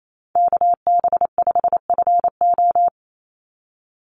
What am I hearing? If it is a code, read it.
Morse code, heard as a keyed tone, about 28 wpm, 700 Hz. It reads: KB5FO
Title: USA Call Signs 28wpm